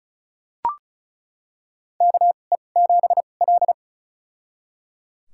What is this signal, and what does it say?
Morse code, heard as a keyed tone, about 36 wpm, spaced as for 25 wpm, 700 Hz. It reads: KE7L
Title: USA Call Signs - 4 Characters - 2x Character Spacing - 35wpm